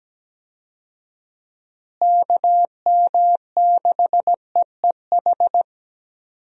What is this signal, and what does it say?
Morse code, heard as a keyed tone, about 17 wpm, 700 Hz. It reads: KM6EEH
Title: USA Call Signs - 6 Characters - Repeated 2x - 17wpm